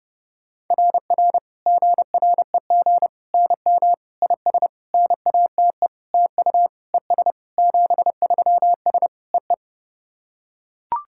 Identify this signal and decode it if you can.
Morse code, heard as a keyed tone, about 32 wpm, 700 Hz. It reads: RR GREG NM IS NATE TU ES 73S EE
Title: SOTA QSO Practice 30wpm